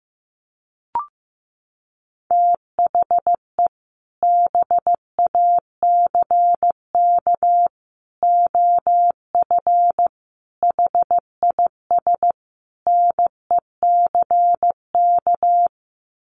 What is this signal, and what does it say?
Morse code, heard as a keyed tone, about 15 wpm, 700 Hz. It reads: THE BACK OF HIS NECK
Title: Sets of 5 Words Encore 15wpm